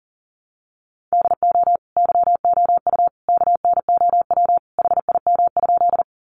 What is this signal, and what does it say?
Morse code, heard as a keyed tone, about 40 wpm, 700 Hz. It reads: DO YOU KNOW HIM?
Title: Sentences from Top 100 Words 40wpm